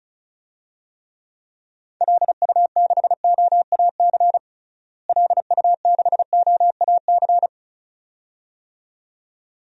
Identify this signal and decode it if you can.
Morse code, heard as a keyed tone, about 34 wpm, 700 Hz. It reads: LU6OAC LU6OAC
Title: DXCC-100 Call Signs - 6 Characters - Repeated 2x - 35wpm